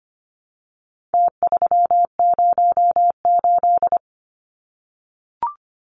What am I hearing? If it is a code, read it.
Morse code, heard as a keyed tone, about 25 wpm, 700 Hz. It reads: T308
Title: Numbers - Contest Style - 4-Digit 25wpm